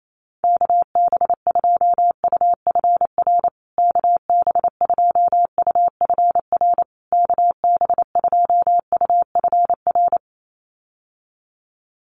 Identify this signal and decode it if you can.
Morse code, heard as a keyed tone, about 28 wpm, 700 Hz. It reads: KB2UFR KB2UFR KB2UFR
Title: USA Call Signs Repeated 3x 28wpm